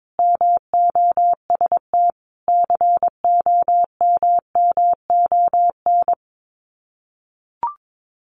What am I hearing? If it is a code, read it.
Morse code, heard as a keyed tone, about 22 wpm, 700 Hz. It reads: MOST COMMON
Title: Sets of 5 Words Encore2 22wpm